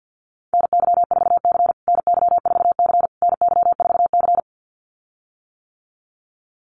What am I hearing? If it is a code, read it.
Morse code, heard as a keyed tone, about 49 wpm, 700 Hz. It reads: NY4C NY4C NY4C
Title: USA Call Signs Repeated 3x 50wpm